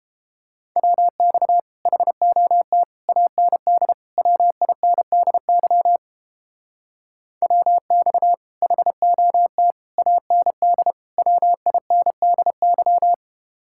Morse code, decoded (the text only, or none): WX HOT AND WINDY WX HOT AND WINDY